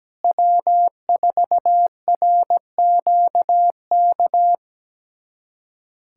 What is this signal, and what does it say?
Morse code, heard as a keyed tone, about 17 wpm, 700 Hz. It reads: W4RQK
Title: USA Callsign DE Callsign - 5 Characters - 17wpm